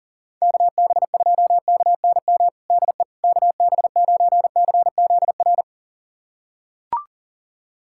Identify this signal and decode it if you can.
Morse code, heard as a keyed tone, about 40 wpm, 700 Hz. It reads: KB2KNM DE KB9CZR
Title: USA Callsign DE Callsign - 6 Characters - 40wpm